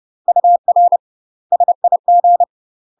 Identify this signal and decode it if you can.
Morse code, heard as a keyed tone, about 30 wpm, 700 Hz. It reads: UR SIG
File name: UK MASTER QSOs TRAINING / 30WPM / UK MASTER QSOs 0036